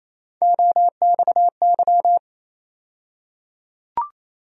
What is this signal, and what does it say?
Morse code, heard as a keyed tone, about 28 wpm, 700 Hz. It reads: OXY
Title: USA Call Sign Postfixes - 3 Letters 28wpm